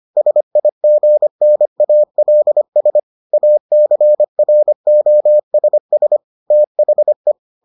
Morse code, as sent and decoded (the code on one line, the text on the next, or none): ... .. --. -. .- .-.. ... / .- -.-. .-. --- ... ... / - .... .
SIGNALS ACROSS THE